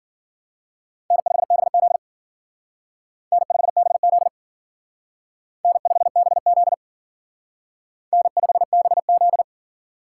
Morse code, heard as a keyed tone, about 51 wpm, 700 Hz. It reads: N5BZ N5BZ N5BZ N5BZ